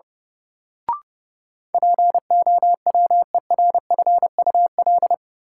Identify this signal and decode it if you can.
Morse code, heard as a keyed tone, about 30 wpm, 700 Hz. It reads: POWERFUL
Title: Sentences from Top 1000 Words - Review Part 6 - New Word Drill - 30wpm